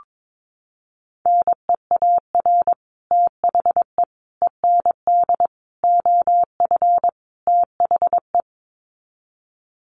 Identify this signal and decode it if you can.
Morse code, heard as a keyed tone, about 22 wpm, 700 Hz. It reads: NEAR THE END OF THE
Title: Sets of 5 Words Encore 22wpm